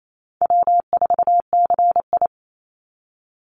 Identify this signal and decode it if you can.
Morse code, heard as a keyed tone, about 28 wpm, 700 Hz. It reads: W4CI
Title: Easy Call Signs Repeated 3x - 28wpm